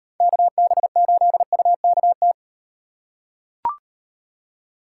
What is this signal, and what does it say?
Morse code, heard as a keyed tone, about 38 wpm, 700 Hz. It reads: KB8UKT